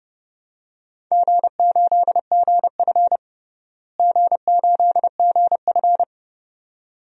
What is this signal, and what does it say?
Morse code, heard as a keyed tone, about 32 wpm, 700 Hz. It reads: G8GF G8GF
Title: DXCC-100 Call Signs - 4 Characters - Repeated 2x - 30wpm